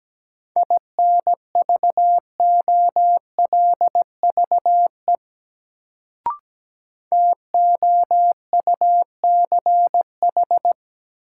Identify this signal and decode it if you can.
Morse code, heard as a keyed tone, about 17 wpm, 700 Hz. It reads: INVOLVE TOUCH
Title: Sentences from Top 1200 Words - New Word Drill - 17wpm